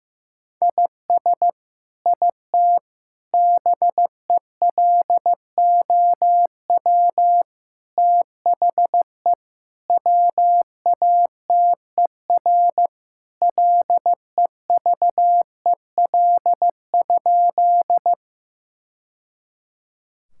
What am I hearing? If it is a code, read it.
Morse code, heard as a keyed tone, about 15 wpm, 700 Hz. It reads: IS IT BELOW THE WATER LEVEL?
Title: Sentences from Top 1000 Words - Review Part 5 - 15wpm